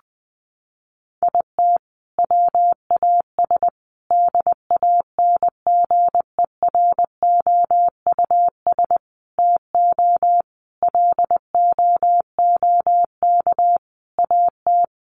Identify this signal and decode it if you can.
Morse code, heard as a keyed tone, about 20 wpm, 700 Hz. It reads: IT WAS DANGEROUS TO LOOK AT
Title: Sentences from Top 1300 Words 20wpm